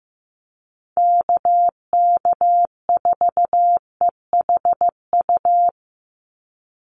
Morse code, heard as a keyed tone, about 15 wpm, 700 Hz. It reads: KK4EHU
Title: USA Call Signs 15wpm